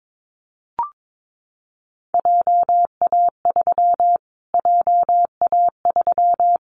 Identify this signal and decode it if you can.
Morse code, heard as a keyed tone, about 22 wpm, 700 Hz. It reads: JA3 JA3
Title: Callsign Prefixes for Top 25 DXCC - Repeated 3x - 22wpm